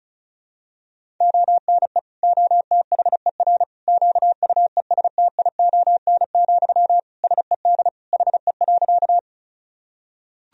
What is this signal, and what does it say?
Morse code, heard as a keyed tone, about 36 wpm, 700 Hz. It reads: ONE OTHER QUESTION, SED HE.
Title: Sentences from Top 400 Words and Abbreviations 35wpm